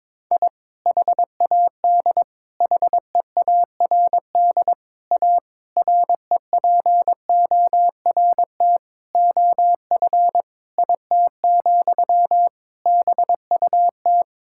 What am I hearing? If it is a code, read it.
Morse code, heard as a keyed tone, about 22 wpm, 700 Hz. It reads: I HAD HEARD A REPORT OF IT, BUT
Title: Sentences from Top 1000 Words - Review Part 6 - 22wpm